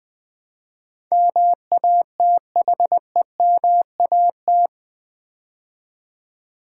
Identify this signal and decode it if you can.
Morse code, heard as a keyed tone, about 20 wpm, 700 Hz. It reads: MATHEMAT
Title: Build A Word - 4 Letters At A Time - 12-Letter Words - 20wpm